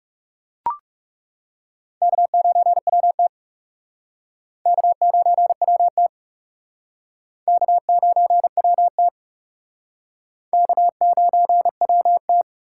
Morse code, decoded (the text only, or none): K9WT K9WT K9WT K9WT